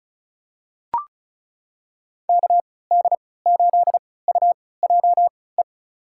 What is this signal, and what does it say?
Morse code, heard as a keyed tone, about 36 wpm, spaced as for 20 wpm, 700 Hz. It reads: KD8UJE
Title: USA Call Signs - 6 Characters - 3x Character Spacing - 35wpm